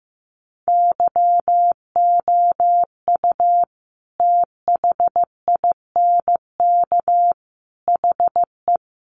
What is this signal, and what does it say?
Morse code, heard as a keyed tone, about 15 wpm, 700 Hz. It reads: YOU THINK HE
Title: Sentences from Top 800 Words 15wpm